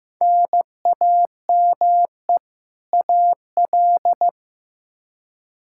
Morse code, heard as a keyed tone, about 15 wpm, 700 Hz. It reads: NAME AL
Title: Practice QSO - Exchange 1 - Beginning - 15wpm